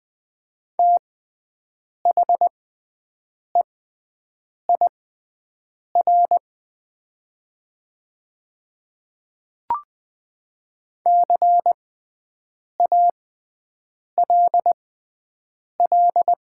Morse code, heard as a keyed tone, about 20 wpm, spaced as for 7 wpm, 700 Hz. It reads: THEIR CALL
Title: 6x Character Spacing - Top 100 Words - 20wpm